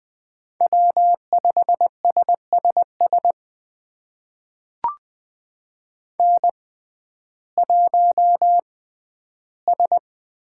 Morse code, heard as a keyed tone, about 20 wpm, spaced as for 12 wpm, 700 Hz. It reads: W5SSS N1S
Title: USA Call Signs - 5 Characters - 6x Character Spacing - 20wpm